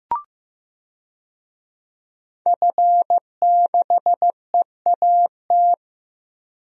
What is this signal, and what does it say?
Morse code, heard as a keyed tone, about 15 wpm, 700 Hz. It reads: F6EAT
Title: DXCC-100 Call Signs - 5 Characters - Repeated 3x - 15wpm